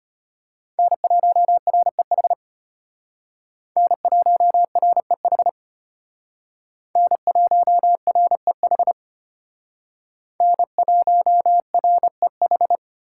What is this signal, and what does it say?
Morse code, heard as a keyed tone, about 32 wpm, 700 Hz. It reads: N1REH N1REH N1REH N1REH